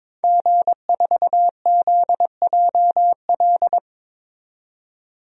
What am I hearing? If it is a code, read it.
Morse code, heard as a keyed tone, about 21 wpm, 700 Hz. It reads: G4ZJL
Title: DXCC-100 Call Signs - 5 Characters - 22wpm